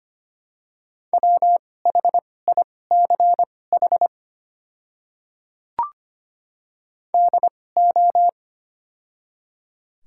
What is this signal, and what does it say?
Morse code, heard as a keyed tone, about 25 wpm, spaced as for 18 wpm, 700 Hz. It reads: WHICH DO
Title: 2x Character Spacing - Top 100 Words - 25wpm